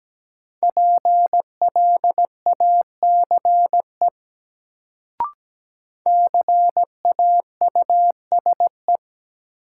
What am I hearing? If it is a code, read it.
Morse code, heard as a keyed tone, about 17 wpm, 700 Hz. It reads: PLACE CAUSE
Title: Sentences from Top 200 Words - New Word Drill - 17wpm